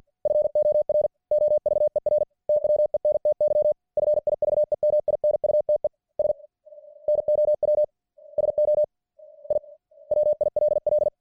Morse code, heard as a keyed tone, about 50 wpm, 600 Hz. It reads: FOR OVER TWENTY FIVE MINUTES NOW SO I WILL